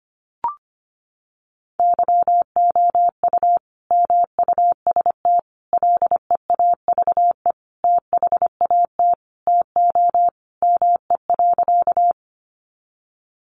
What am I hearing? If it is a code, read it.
Morse code, heard as a keyed tone, about 25 wpm, 700 Hz. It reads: YOU MUST LEAVE THAT TO ME.
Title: Sentences from Top 400 Words 25wpm